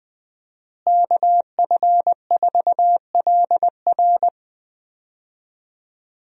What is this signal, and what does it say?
Morse code, heard as a keyed tone, about 21 wpm, 700 Hz. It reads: KF4LR
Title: USA Call Signs - 5 Characters - Repeated 3x - 20wpm